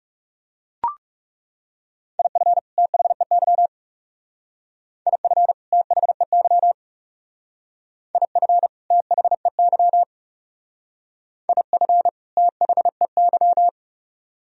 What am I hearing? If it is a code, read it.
Morse code, heard as a keyed tone, about 38 wpm, 700 Hz. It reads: IF THEY IF THEY IF THEY IF THEY